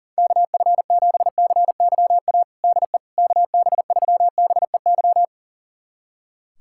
Morse code, heard as a keyed tone, about 40 wpm, 700 Hz. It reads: KF7CYA DE KB3BEY
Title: USA Callsign DE Callsign - 6 Characters - 40wpm